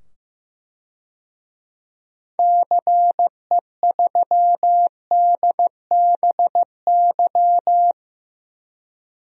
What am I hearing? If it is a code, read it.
Morse code, heard as a keyed tone, about 15 wpm, 700 Hz. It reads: CE3DBY
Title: DXCC-100 Call Signs - 6 Characters - 15wpm